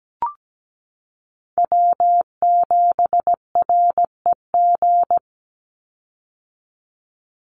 Morse code, heard as a keyed tone, about 17 wpm, 700 Hz. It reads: W7REG
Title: USA Call Signs 17wpm